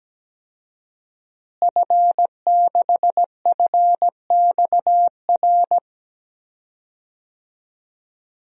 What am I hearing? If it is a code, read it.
Morse code, heard as a keyed tone, about 17 wpm, 700 Hz. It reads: F6FXR
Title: DXCC-100 Call Signs - 5 Characters - Repeated 2x - 17wpm